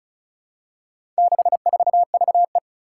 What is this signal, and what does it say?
Morse code, heard as a keyed tone, about 36 wpm, 700 Hz. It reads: 64VE
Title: Four Challenging Letter-Number Combinations 35wpm